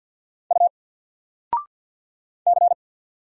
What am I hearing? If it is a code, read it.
Morse code, heard as a keyed tone, about 51 wpm, 700 Hz. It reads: U C
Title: Single Letter-Number Alt 50wpm